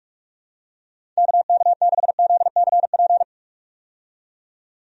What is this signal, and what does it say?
Morse code, heard as a keyed tone, about 46 wpm, 700 Hz. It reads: KK6ZCP
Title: USA Call Signs - 6 Characters - 45wpm